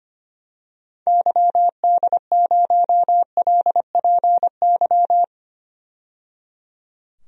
Morse code, heard as a keyed tone, about 25 wpm, 700 Hz. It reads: YD0LPY
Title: DXCC-100 Call Signs - 6 Characters - 25wpm